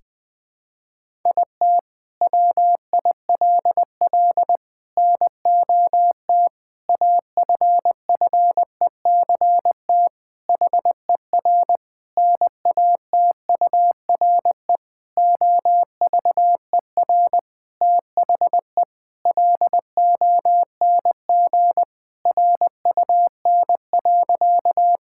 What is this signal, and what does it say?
Morse code, heard as a keyed tone, about 20 wpm, 700 Hz. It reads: IT WILL NOT AFFECT HER NATURE OVER THE LONG RUN.
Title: Sentences from Top 1000 Words - Review Part 6 - 20wpm